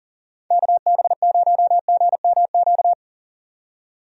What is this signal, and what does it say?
Morse code, heard as a keyed tone, about 40 wpm, 700 Hz. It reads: KB0GMQ
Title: USA Call Signs - 6 Characters - 40wpm